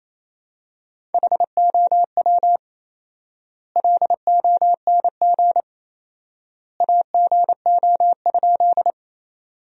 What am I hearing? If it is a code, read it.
Morse code, heard as a keyed tone, about 28 wpm, 700 Hz. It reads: HOW LONG AGO?
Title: 4x Word-Spacing - Sentences from Top 500 Words - 28wpm